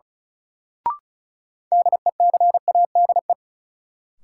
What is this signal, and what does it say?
Morse code, heard as a keyed tone, about 36 wpm, 700 Hz. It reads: DECADE
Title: Sets of 5 Words - New Word Drill - 35wpm